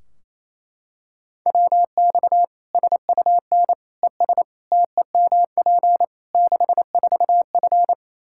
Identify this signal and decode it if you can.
Morse code, heard as a keyed tone, about 28 wpm, 700 Hz. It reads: WX SUN ES TEMP 64F
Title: Practice QSO - Exchange 2 - Info - 28wpm